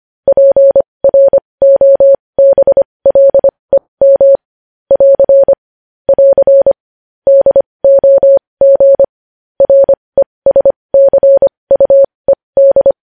Morse code, heard as a keyed tone, about 25 wpm, spaced as for 20 wpm, 550 Hz. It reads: PROBLEM <AR> <AR> DOG RESCUED
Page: news-headlines-in-morse-code-at-20-wpm